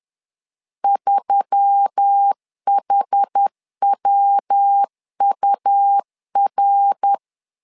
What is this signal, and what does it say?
Morse code, heard as a keyed tone, about 10 wpm, 800 Hz. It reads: 3HWUR
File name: LEZIONI / 4 Q1S3 10wpm 2427